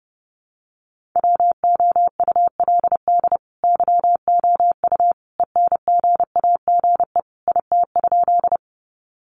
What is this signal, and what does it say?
Morse code, heard as a keyed tone, about 30 wpm, 700 Hz. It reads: WOULD YOU ENGAGE IT?
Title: Sentences from Top 1500 Words 30wpm